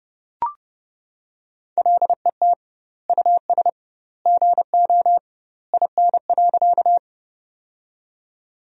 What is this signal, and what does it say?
Morse code, heard as a keyed tone, about 30 wpm, 700 Hz. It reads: LET US GO IN.
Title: 2x Word-Spacing - Sentences from Top 300 Words - 30wpm